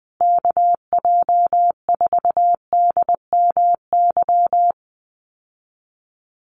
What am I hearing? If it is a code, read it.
Morse code, heard as a keyed tone, about 20 wpm, 700 Hz. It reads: KJ4DMY
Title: USA Call Signs Repeated 2x 20wpm